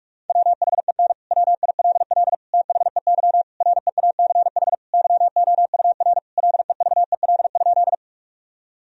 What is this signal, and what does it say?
Morse code, heard as a keyed tone, about 44 wpm, 700 Hz. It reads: WHEN WILL THEY REACH YOUR LEVEL?
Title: Sentences from Top 1000 Words - Review Part 4 - 45wpm